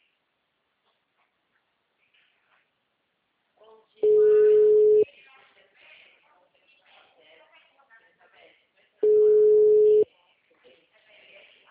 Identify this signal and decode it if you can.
Ringback tone (Europe, 425 Hz), with speech.